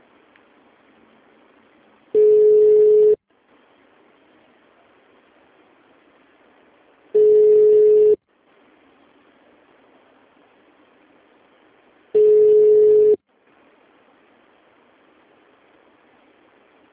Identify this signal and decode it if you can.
Ringback tone (Europe, 425 Hz).
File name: out- -2003- - - .2